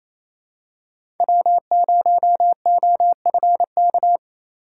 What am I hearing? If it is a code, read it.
Morse code, heard as a keyed tone, about 28 wpm, 700 Hz. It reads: W0OFK
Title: USA Call Signs - 6x Character Spacing - 28wpm